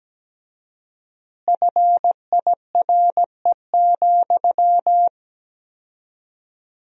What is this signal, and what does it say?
Morse code, heard as a keyed tone, about 17 wpm, 700 Hz. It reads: FIRE,
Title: 4x Word-Spacing - Sentences from Top 400 Words - 17wpm